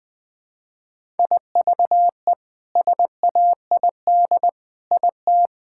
Morse code, heard as a keyed tone, about 21 wpm, 700 Hz. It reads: IVE SAID IT
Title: Sets of 8 Words - 20wpm